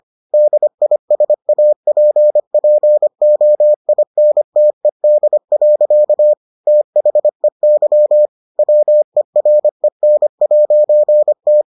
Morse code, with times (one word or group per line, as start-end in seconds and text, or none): 0.3-6.3 DISAPPOINTED.
6.7-8.3 THEY
8.6-11.6 WEREN'T